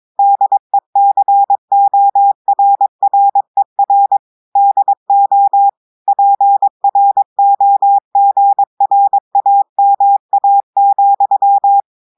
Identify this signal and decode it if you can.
Morse code, heard as a keyed tone, about 22 wpm, 800 Hz. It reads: DECORRER DO PROGRAMA,
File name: audios / portugais / portugais texte en clair 2 vitesse 22 mots minutes